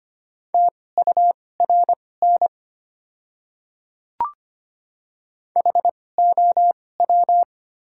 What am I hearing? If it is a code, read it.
Morse code, heard as a keyed tone, about 25 wpm, spaced as for 18 wpm, 700 Hz. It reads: TURN HOW
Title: 2x Character Spacing - Top 200 Words - 25wpm